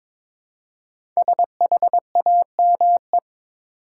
Morse code, heard as a keyed tone, about 22 wpm, 700 Hz. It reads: SHAME
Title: Five Letter Words - 22wpm